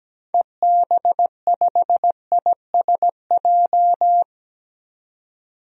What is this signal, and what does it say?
Morse code, heard as a keyed tone, about 17 wpm, 700 Hz. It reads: EB5ISJ
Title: DXCC-100 Call Signs - 6 Characters - 17wpm